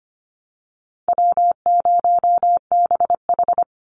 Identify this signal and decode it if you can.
Morse code, heard as a keyed tone, about 25 wpm, 700 Hz. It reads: W0BH
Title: Easy Call Signs Repeated 2x - 25wpm